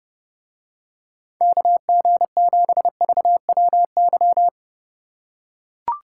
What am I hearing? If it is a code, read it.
Morse code, heard as a keyed tone, about 29 wpm, 700 Hz. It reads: KG7VWY